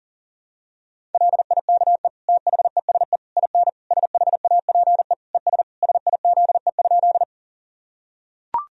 Morse code, heard as a keyed tone, about 40 wpm, 700 Hz. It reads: LIKE THESE IN SHAPE ES SIZE?
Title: Sentences from Top 500 Words and Abbreviations 40wpm